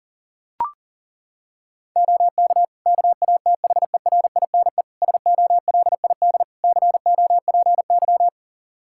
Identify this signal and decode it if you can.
Morse code, heard as a keyed tone, about 40 wpm, 700 Hz. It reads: OK KATHERINE SOLID COPY
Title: Practice QSO - Exchange 3 - More Info - 40wpm